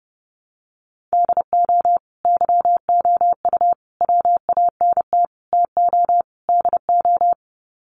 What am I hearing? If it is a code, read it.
Morse code, heard as a keyed tone, about 30 wpm, 700 Hz. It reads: DO YOU WANT TO DO
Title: Sets of 5 Words Encore 30wpm